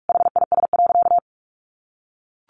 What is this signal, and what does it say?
Morse code, heard as a keyed tone, about 44 wpm, 700 Hz. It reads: HIS.
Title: Sentences from Top 900 Words 45wpm